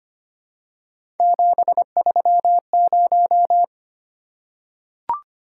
Morse code, heard as a keyed tone, about 25 wpm, 700 Hz. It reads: 730
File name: Three Letter-Number Combinations - 25wpm